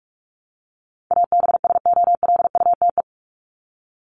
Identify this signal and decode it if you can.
Morse code, heard as a keyed tone, about 44 wpm, 700 Hz. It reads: ABSOLUTE
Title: LY Suffix - Top 100 Words with Base Words 45wpm